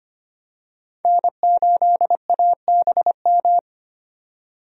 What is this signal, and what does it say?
Morse code, heard as a keyed tone, about 25 wpm, 700 Hz. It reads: N8ABM
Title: USA Call Signs - 5 Characters - 25wpm